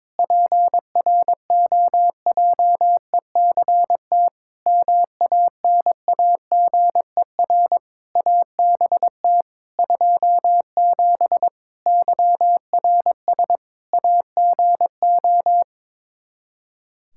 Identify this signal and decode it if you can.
Morse code, heard as a keyed tone, about 22 wpm, 700 Hz. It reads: PROJECT MANAGER ABT 27 YRS AGO
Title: Practice QSO - Exchange 3 - More Info - 22wpm